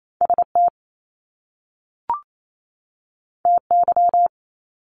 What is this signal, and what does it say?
Morse code, heard as a keyed tone, about 28 wpm, 700 Hz. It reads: ST TY
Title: Top Two Letter Combinations in Words 28wpm